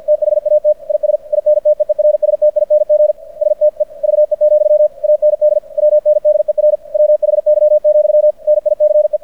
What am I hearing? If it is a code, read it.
Morse code, heard as a keyed tone, about 50 wpm, 600 Hz. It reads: THAT IS INTEERSTING STEVE, AND WNDER WHY, NICE
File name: archived / CRSnet